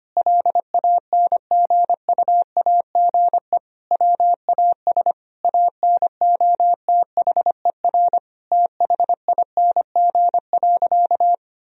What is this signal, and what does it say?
Morse code, heard as a keyed tone, about 25 wpm, 700 Hz. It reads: LANGUAGE WAS ANOTHER THING.
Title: Sentences from Top 1000 Words - Review Part 3 - 25wpm